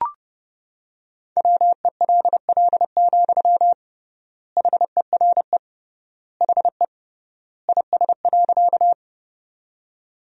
Morse code, heard as a keyed tone, about 30 wpm, 700 Hz. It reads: WELL, HERE HE IS.
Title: 3x Word-Spacing - Sentences from Top 200 Words - 30wpm